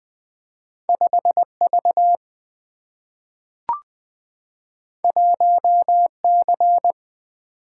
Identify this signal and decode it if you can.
Morse code, heard as a keyed tone, about 20 wpm, 700 Hz. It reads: 5V 1C
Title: Two Letter-Number Combinations - 20wpm